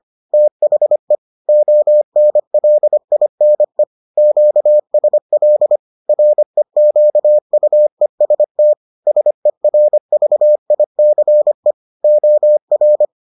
Morse code, heard as a keyed tone, about 25 wpm, 600 Hz. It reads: THE ONLINE QSL REQUEST SERVICE OR